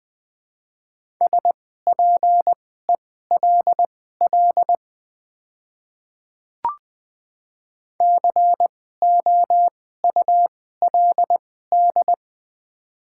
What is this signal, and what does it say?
Morse code, heard as a keyed tone, about 20 wpm, spaced as for 14 wpm, 700 Hz. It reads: SPELL COULD
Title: 2x Character Spacing - Top 200 Words - 20wpm